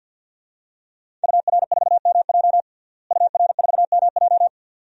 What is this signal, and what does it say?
Morse code, heard as a keyed tone, about 52 wpm, 700 Hz. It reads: UR4MJ UR4MJ
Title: DXCC-100 Call Signs - 5 Characters - Repeated 2x - 50wpm